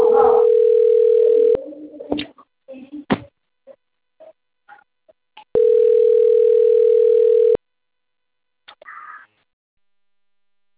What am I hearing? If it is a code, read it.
Ringback tone (North America), 440 + 480 Hz.